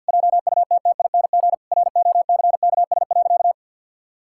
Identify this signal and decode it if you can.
Morse code, heard as a keyed tone, about 50 wpm, 700 Hz. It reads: JUTTING ROCKS.
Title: Pangram Challenge v2 - 50wpm